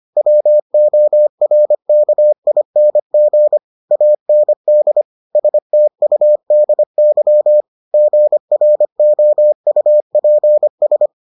Morse code, heard as a keyed tone, about 25 wpm, 600 Hz. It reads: WORKING AND STUDY GROUPS